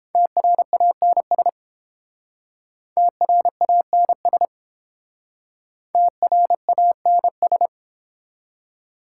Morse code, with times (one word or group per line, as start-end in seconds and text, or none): 0.2-1.5 TRANS
3.0-4.5 TRANS
5.9-7.7 TRANS